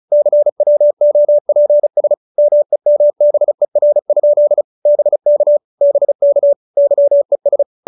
Morse code, heard as a keyed tone, about 34 wpm, 590 Hz. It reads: CWOPS MEMBER? BK BK YES